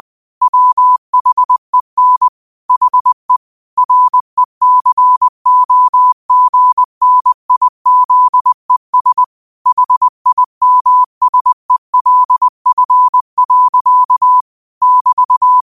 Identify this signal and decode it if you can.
Morse code, heard as a keyed tone, about 20 wpm, 1000 Hz. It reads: WHEN HE RECOGNIZES HIMSELF. <BT>
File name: qotdfiles / 20wpm